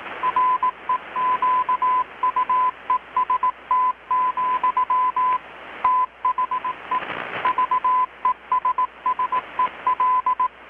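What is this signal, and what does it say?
Morse code, heard as a keyed tone, about 18 wpm, 1010 Hz. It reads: REQUEST, THE VESSEL
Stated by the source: Morse from KPH's Night of Nights XX maritime radiotelegraphy event, received in Chiapas, Mexico about 3,800 km away on a Kenwood R-600 with a grounded 10 m long-wire antenna.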